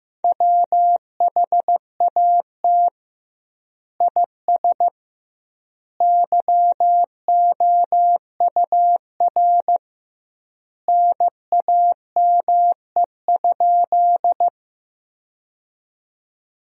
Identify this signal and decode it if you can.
Morse code, heard as a keyed tone, about 15 wpm, 700 Hz. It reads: WHAT IS YOUR NAME?
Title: 2x Word-Spacing - Sentences from Top 200 Words - 15wpm